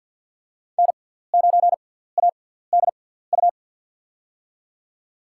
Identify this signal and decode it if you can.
Morse code, heard as a keyed tone, about 50 wpm, 700 Hz. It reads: N 9 A D U
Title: USA Call Signs - 5 Characters - 6x Character Spacing - 50wpm